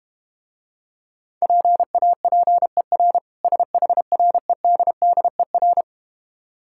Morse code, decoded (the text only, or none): PAPER SHREDDER